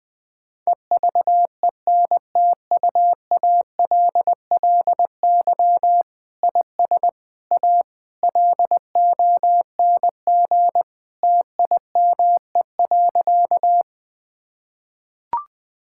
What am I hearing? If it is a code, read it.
Morse code, heard as a keyed tone, about 20 wpm, 700 Hz. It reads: EVENTUALLY IS A LONG TIME.
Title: Sentences from Top 1000 Words - Review Part 6 - 20wpm